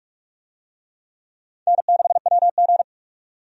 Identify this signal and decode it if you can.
Morse code, heard as a keyed tone, about 44 wpm, 700 Hz. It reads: N6WG
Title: USA Call Signs - 4 Characters - 45wpm